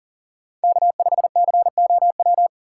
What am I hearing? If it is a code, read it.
Morse code, heard as a keyed tone, about 40 wpm, 700 Hz. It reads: K5COW
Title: USA Call Signs - 2x Character Spacing - 40wpm